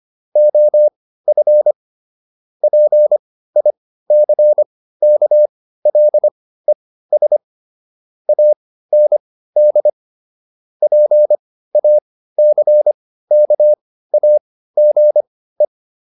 Morse code, heard as a keyed tone, about 25 wpm, spaced as for 15 wpm, 600 Hz. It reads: OF PICKLES AND PACKAGE